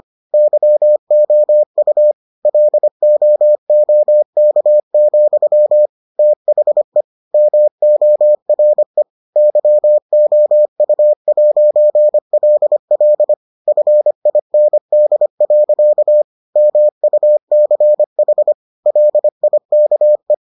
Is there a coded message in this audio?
Yes: Morse code from 0.3 to 20.3 s, heard as a keyed tone, about 25 wpm, 600 Hz. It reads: YOU LOOK, THE MORE YOU'LL FIND. MUCH LIKE